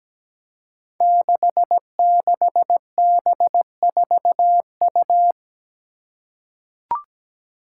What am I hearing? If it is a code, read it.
Morse code, heard as a keyed tone, about 17 wpm, 700 Hz. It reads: 66B4U
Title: Five Challenging Letter-Number Combinations 17wpm